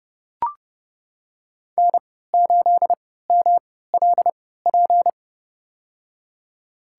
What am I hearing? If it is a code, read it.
Morse code, heard as a keyed tone, about 29 wpm, spaced as for 17 wpm, 700 Hz. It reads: N8MLP
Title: USA Call Signs - 3x Character Spacing - 30wpm